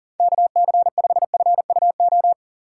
Morse code, heard as a keyed tone, about 40 wpm, 700 Hz. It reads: KC5FUO
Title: USA Call Signs - 6 Characters - 2x Character Spacing - 40wpm